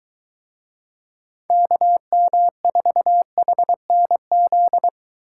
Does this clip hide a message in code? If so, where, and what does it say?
Yes: Morse code from 1.5 to 4.9 s, heard as a keyed tone, about 21 wpm, 700 Hz. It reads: KM4HNZ